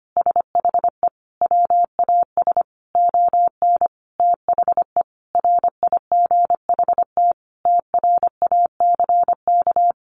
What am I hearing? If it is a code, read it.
Morse code, heard as a keyed tone, about 25 wpm, 690 Hz. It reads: SHE WAS ON THE RIGHT TRACK
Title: Sentences from Top 1100 Words 25wpm